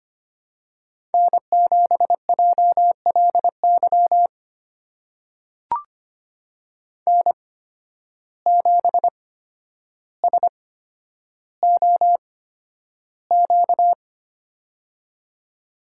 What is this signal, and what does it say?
Morse code, heard as a keyed tone, about 25 wpm, spaced as for 11 wpm, 700 Hz. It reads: N7JLY N7SOQ
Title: USA Call Signs - 5 Characters - 8x Character Spacing - 25wpm